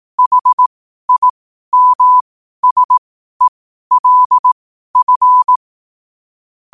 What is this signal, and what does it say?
Morse code, heard as a keyed tone, about 18 wpm, spaced as for 12 wpm, 1000 Hz. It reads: HIMSELF
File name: qotdfiles / 12wpm